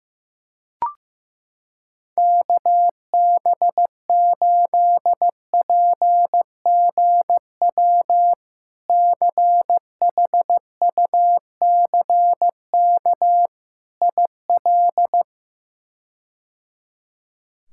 Morse code, heard as a keyed tone, about 15 wpm, 700 Hz. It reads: KB8PGW CHUCK IL
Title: K1USN SST QSO Practice - 15wpm